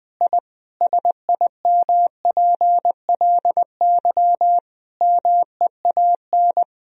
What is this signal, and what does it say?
Morse code, heard as a keyed tone, about 20 wpm, 700 Hz. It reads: I SIMPLY MEAN
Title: Sentences from Top 1000 Words - Review Part 4 - 20wpm